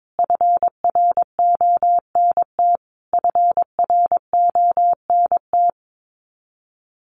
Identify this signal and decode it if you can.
Morse code, heard as a keyed tone, about 22 wpm, 700 Hz. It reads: FRONT FRONT
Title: Top 500 Words Repeated 22wpm